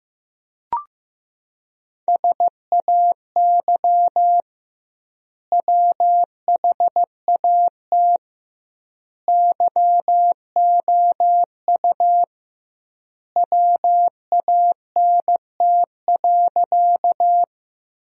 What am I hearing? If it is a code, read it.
Morse code, heard as a keyed tone, about 15 wpm, 700 Hz. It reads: SAY WHAT YOU WANT.
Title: 2x Word-Spacing - Sentences from Top 200 Words - 15wpm